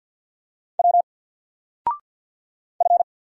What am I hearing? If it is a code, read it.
Morse code, heard as a keyed tone, about 50 wpm, 700 Hz. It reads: W F
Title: Single Letter-Number 50wpm